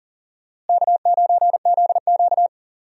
Morse code, heard as a keyed tone, about 40 wpm, 700 Hz. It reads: K9ZQ
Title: USA Call Signs - 4 Characters - Repeated 2x - 40wpm